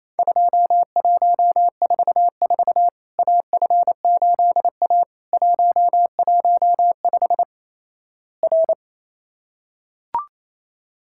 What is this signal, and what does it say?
Morse code, heard as a keyed tone, about 28 wpm, 700 Hz. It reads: 2144 AF8A 115 R
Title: Worked All Europe DX Contest - QTC Exchange - 28wpm